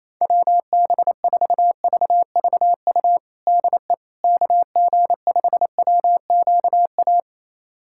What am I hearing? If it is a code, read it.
Morse code, heard as a keyed tone, about 28 wpm, 700 Hz. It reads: WB4VVU DE KG5WQA
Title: USA Callsign DE Callsign - 6 Characters - 28wpm